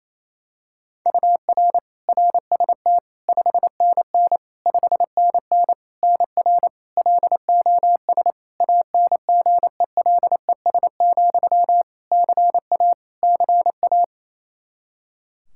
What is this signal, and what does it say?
Morse code, heard as a keyed tone, about 28 wpm, 700 Hz. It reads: UR RST 5NN 5NN NR LOS ANGELES, CA CA
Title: K1USN SST QSO Practice 28wpm